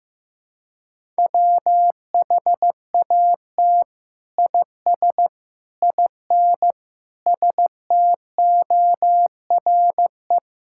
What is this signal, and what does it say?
Morse code, heard as a keyed tone, about 15 wpm, 700 Hz. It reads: WHAT IS IN STORE
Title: Sentences from Top 1000 Words - Review Part 5 - 15wpm